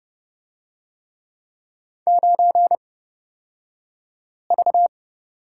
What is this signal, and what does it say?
Morse code, heard as a keyed tone, about 30 wpm, spaced as for 5 wpm, 700 Hz. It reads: 9V
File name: Lesson 38d - Letter q - Two Characters 30wpm